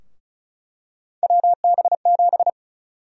Morse code, heard as a keyed tone, about 36 wpm, 700 Hz. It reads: WB7
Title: USA Call Sign Prefixes - 35wpm